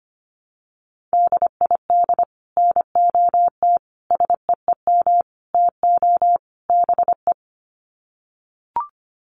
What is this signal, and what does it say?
Morse code, heard as a keyed tone, about 25 wpm, 700 Hz. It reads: DID NOT SEEM TO BE
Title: Sets of 5 Words Encore 25wpm